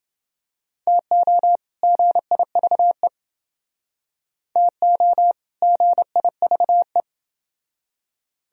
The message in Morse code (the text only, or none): TO GIVE TO GIVE